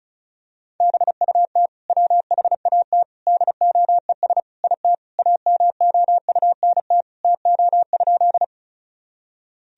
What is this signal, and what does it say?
Morse code, heard as a keyed tone, about 34 wpm, 700 Hz. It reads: BUT WHAT DOES IT AMOUNT TO?
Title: Sentences from Top 1000 Words - Review Part 4 - 35wpm